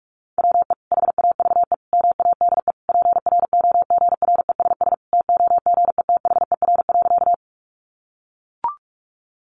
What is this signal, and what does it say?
Morse code, heard as a keyed tone, about 44 wpm, 700 Hz. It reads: WE HAVE MADE PROGRESS TOGETHER.
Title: Sentences from Top 1400 Words 45wpm